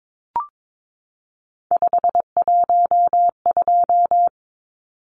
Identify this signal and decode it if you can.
Morse code, heard as a keyed tone, about 22 wpm, 700 Hz. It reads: 512
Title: Numbers - 3-Digit 22wpm